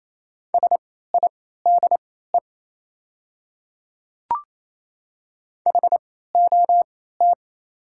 Morse code, heard as a keyed tone, about 28 wpm, spaced as for 16 wpm, 700 Hz. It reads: SIDE HOT
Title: 3x Character Spacing - Top 100 Words - 28wpm